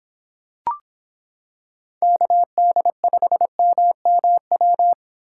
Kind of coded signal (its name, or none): Morse code